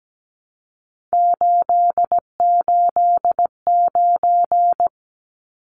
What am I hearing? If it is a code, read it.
Morse code, heard as a keyed tone, about 17 wpm, 700 Hz. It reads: 889
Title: Numbers - 3-Digit 17wpm